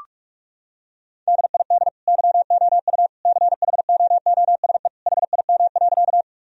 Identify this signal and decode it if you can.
Morse code, heard as a keyed tone, about 46 wpm, 700 Hz. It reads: DID YOU CHOOSE HIM.
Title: Sentences from Top 1000 Words - Review Part 4 - 45wpm